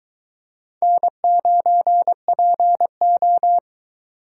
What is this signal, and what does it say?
Morse code, heard as a keyed tone, about 22 wpm, 700 Hz. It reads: N9PO